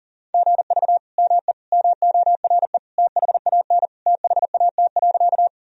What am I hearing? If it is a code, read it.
Morse code, heard as a keyed tone, about 37 wpm, 700 Hz. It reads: GV ME MORE THAN THAT.
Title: Sentences from Top 200 Words and Abbreviations 40wpm